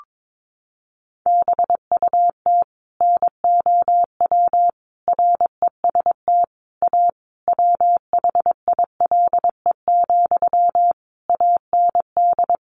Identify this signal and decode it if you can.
Morse code, heard as a keyed tone, about 22 wpm, 700 Hz. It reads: BUT NOW REST A WHILE, AND
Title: Sentences from Top 1100 Words 22wpm